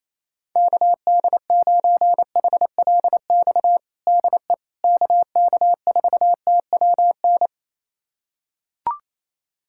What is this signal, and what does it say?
Morse code, heard as a keyed tone, about 28 wpm, 700 Hz. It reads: KD9HLX DE KK4TWN
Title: USA Callsign DE Callsign - 6 Characters - 28wpm